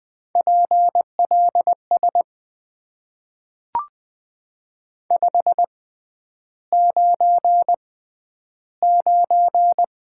Morse code, heard as a keyed tone, about 20 wpm, spaced as for 10 wpm, 700 Hz. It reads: PLS 599
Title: Top 90 QSO Elements - 6x Character Spacing - 20wpm